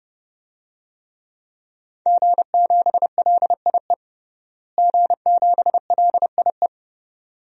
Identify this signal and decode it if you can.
Morse code, heard as a keyed tone, about 30 wpm, 700 Hz. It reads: G7LIE G7LIE
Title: DXCC-100 Call Signs - 5 Characters - Repeated 2x - 30wpm